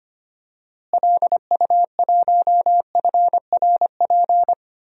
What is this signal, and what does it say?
Morse code, heard as a keyed tone, about 25 wpm, 700 Hz. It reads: LU1FRP
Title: DXCC-100 Call Signs - 6 Characters - Repeated 2x - 25wpm